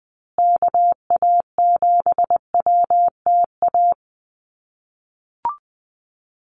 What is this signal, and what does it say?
Morse code, heard as a keyed tone, about 20 wpm, 700 Hz. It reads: KA7WTA
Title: USA Call Signs Repeated 3x 20wpm